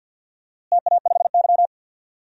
Morse code, heard as a keyed tone, about 48 wpm, 700 Hz. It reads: TA5Y
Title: DXCC-100 Call Signs - 4 Characters - Repeated 3x - 50wpm